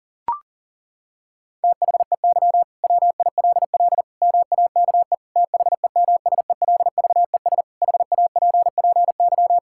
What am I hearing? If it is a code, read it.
Morse code, heard as a keyed tone, about 40 wpm, 700 Hz. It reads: THEY WILL MAKE THEMSELVES HAPPY
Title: Sentences from Top 1600 Words 40wpm